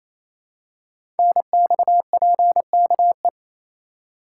Morse code, heard as a keyed tone, about 27 wpm, 700 Hz. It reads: NXPKE
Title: Code Groups - 5 Letters - 28wpm